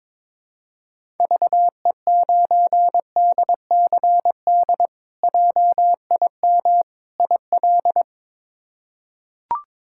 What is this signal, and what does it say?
Morse code, heard as a keyed tone, about 21 wpm, 700 Hz. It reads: VE9DCD JIM IL
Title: K1USN SST QSO Practice - 22wpm